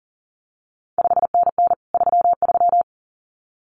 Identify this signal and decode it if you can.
Morse code, heard as a keyed tone, about 42 wpm, 700 Hz. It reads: 5NN 33
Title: CQ WW DX Contest Exchange Encore - 40wpm